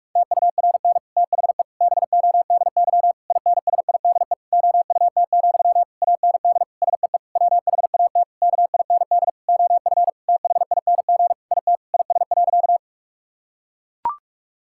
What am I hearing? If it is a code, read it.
Morse code, heard as a keyed tone, about 41 wpm, 700 Hz. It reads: TURN THE BODY INSIDE OUT, AND SEE WHAT KIND OF THING IT IS.
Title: Sentences from Top 1000 Words - Review Part 6 - 45wpm